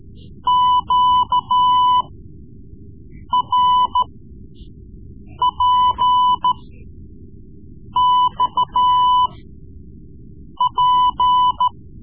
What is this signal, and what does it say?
Morse code, heard as a keyed tone, about 14 wpm, spaced as for 5 wpm, 1000 Hz. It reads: QRPXP
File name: Morse Lessons / Morse Lessons / Tape-3B